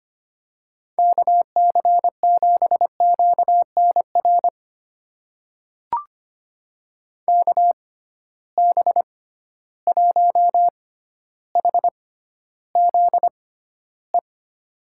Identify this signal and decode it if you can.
Morse code, heard as a keyed tone, about 25 wpm, spaced as for 13 wpm, 700 Hz. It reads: KC7QNR KB1HZE
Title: USA Call Signs - 6 Characters - 6x Character Spacing - 25wpm